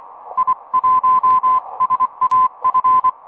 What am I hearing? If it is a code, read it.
Morse code, heard as a keyed tone, about 24 wpm, 1020 Hz. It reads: I1SAF
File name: attachments